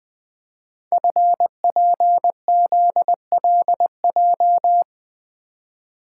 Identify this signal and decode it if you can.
Morse code, heard as a keyed tone, about 20 wpm, 700 Hz. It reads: FPZLJ
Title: Code Groups - 5 Letter:Numbers - 20wpm